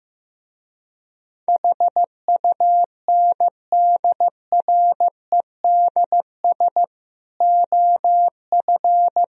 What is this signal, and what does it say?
Morse code, heard as a keyed tone, about 15 wpm, 700 Hz. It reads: HUNDREDS OF
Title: Speed-Racing - Sets of 5 Words 15wpm